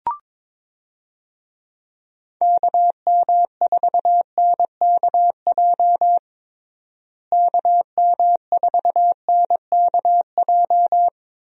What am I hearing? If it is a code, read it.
Morse code, heard as a keyed tone, about 22 wpm, 700 Hz. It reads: KM4NKJ KM4NKJ
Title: USA Call Signs - 6 Characters - Repeated 2x - 22wpm